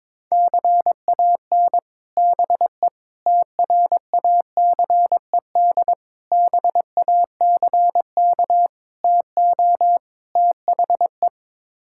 Morse code, heard as a keyed tone, about 22 wpm, 700 Hz. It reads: CAN BE TRACED BACK TO THE
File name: practice / 22wpm / Sets of 6 Words - 22wpm